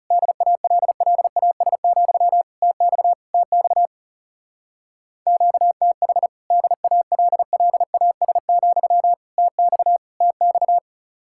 Morse code, heard as a keyed tone, about 36 wpm, 700 Hz. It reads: DALLAS, TX TX QTH DALLAS, TX TX
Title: Speed-Racing Common QSO Phrases - 30wpm